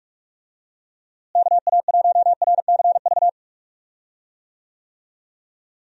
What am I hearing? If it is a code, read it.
Morse code, heard as a keyed tone, about 46 wpm, 700 Hz. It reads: KA1RCV
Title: USA Call Signs - 6 Characters - Repeated 2x - 45wpm